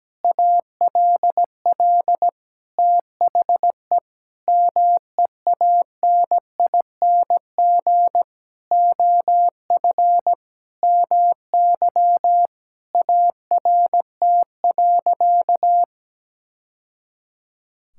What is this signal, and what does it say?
Morse code, heard as a keyed tone, about 17 wpm, 700 Hz. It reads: ALL THE MEANING OF MY ART.
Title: Sentences from Top 1000 Words - Review Part 6 - 17wpm